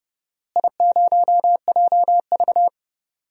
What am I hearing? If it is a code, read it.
Morse code, heard as a keyed tone, about 30 wpm, 700 Hz. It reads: I0JV
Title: DXCC-100 Call Signs - 4 Characters - Repeated 2x - 30wpm